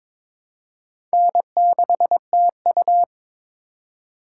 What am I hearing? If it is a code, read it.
Morse code, heard as a keyed tone, about 22 wpm, 700 Hz. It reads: N6TU
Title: USA Call Signs - 4 Characters - 2x Character Spacing - 22wpm